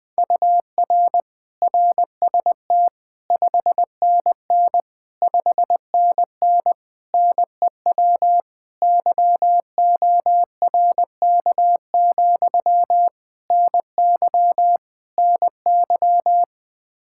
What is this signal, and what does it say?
Morse code, heard as a keyed tone, about 20 wpm, 700 Hz. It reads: UR RST 5NN 5NN NEW YORK, NY NY
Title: Practice QSO - Exchange 1 - Beginning - 20wpm